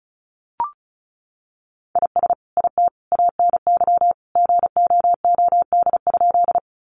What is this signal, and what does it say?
Morse code, heard as a keyed tone, about 34 wpm, 700 Hz. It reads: IS IT ANY GOOD?
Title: Sentences from Top 200 Words 35wpm